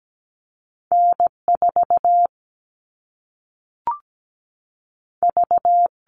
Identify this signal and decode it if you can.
Morse code, heard as a keyed tone, about 17 wpm, 700 Hz. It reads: N4 V
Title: All Cut Numbers - 2-Digit 17wpm